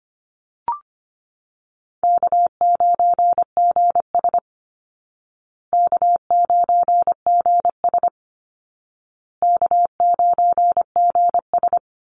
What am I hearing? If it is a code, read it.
Morse code, heard as a keyed tone, about 25 wpm, 700 Hz. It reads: K9GS K9GS K9GS
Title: Call Signs Repeated 2x - 25wpm